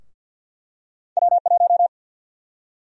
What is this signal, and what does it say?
Morse code, heard as a keyed tone, about 52 wpm, 700 Hz. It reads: W1
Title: USA Call Sign Prefixes - Repeated 2x - 50wpm